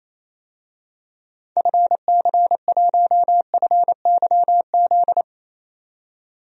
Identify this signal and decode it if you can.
Morse code, heard as a keyed tone, about 28 wpm, 700 Hz. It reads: FC1FYZ
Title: DXCC-100 Call Signs - 6 Characters - 28wpm